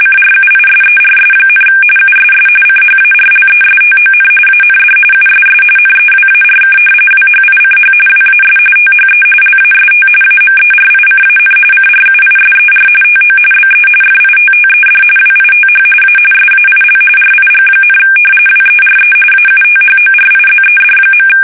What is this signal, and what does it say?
RTTY, about 45.45 baud. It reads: CRUISERS, USS CHARLESTON AND USS NEWARK WHICH WAS THE LAST AMERICAN CRUISER TO BE FITTED WITH A SAIL RIG, WERE AUTHORIZED.
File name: fsk850 / usnavy3